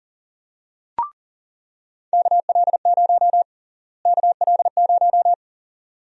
Morse code, heard as a keyed tone, about 40 wpm, 700 Hz. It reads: KL0 KL0
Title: USA Call Sign Prefixes - Repeated 2x - 40wpm